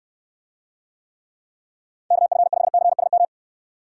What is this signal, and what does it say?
Morse code, heard as a keyed tone, about 69 wpm, 700 Hz. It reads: KF5CSM